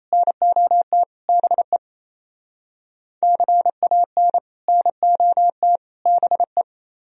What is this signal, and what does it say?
Morse code, heard as a keyed tone, about 33 wpm, 700 Hz. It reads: NOT BE CAN NOT BE
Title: Speed-Racing - Sets of 3 Words 28wpm